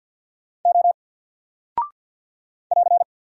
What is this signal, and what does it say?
Morse code, heard as a keyed tone, about 50 wpm, 700 Hz. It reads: O <AR>
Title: Single Character (Letter:Number:Punctuation:Prosign) 50wpm